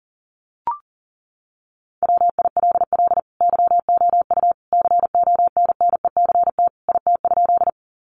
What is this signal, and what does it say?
Morse code, heard as a keyed tone, about 40 wpm, 700 Hz. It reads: WILL YOU CONNECT IT?
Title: Sentences from Top 1500 Words 40wpm